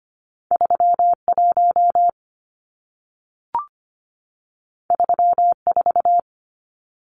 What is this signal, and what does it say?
Morse code, heard as a keyed tone, about 25 wpm, 700 Hz. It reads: 31 34
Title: Numbers - 2-Digit 25wpm